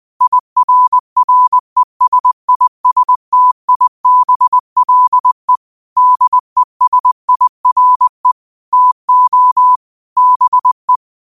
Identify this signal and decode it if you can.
Morse code, heard as a keyed tone, about 20 wpm, 1000 Hz. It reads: IRRESISTIBLE DESIRE TO BE
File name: qotdfiles / 20wpm